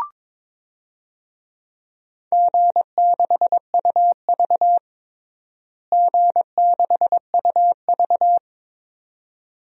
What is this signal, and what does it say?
Morse code, heard as a keyed tone, about 22 wpm, 700 Hz. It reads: G6UV G6UV
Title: DXCC-100 Call Signs - 4 Characters - Repeated 2x - 22wpm